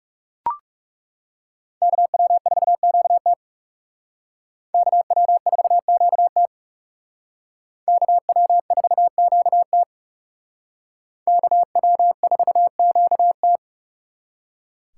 Morse code, heard as a keyed tone, about 36 wpm, 700 Hz. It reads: KW4QT KW4QT KW4QT KW4QT
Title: Speed Racing - USA Call Signs 30wpm